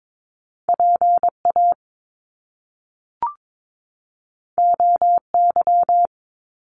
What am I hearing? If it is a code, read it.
Morse code, heard as a keyed tone, about 22 wpm, 700 Hz. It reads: PA OY
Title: Top Two Letter Combinations in Words 22wpm